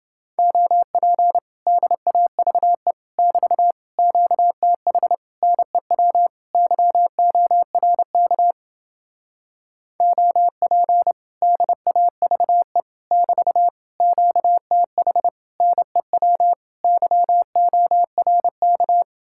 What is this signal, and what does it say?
Morse code, heard as a keyed tone, about 28 wpm, 700 Hz. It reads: OP DAVE <BT> QTH NEW YORK OP DAVE <BT> QTH NEW YORK